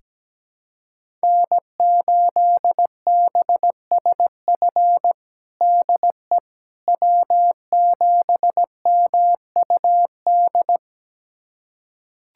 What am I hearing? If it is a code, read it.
Morse code, heard as a keyed tone, about 17 wpm, 700 Hz. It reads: N8BSF DE W7MUD
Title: USA Callsign DE Callsign - 5 Characters - 17wpm